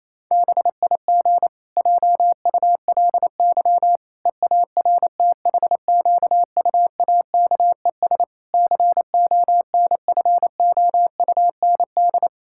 Morse code, heard as a keyed tone, about 28 wpm, 700 Hz. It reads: BIG JULY EARTHQUAKES CONFOUND
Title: Pangram Challenge v2 - 28wpm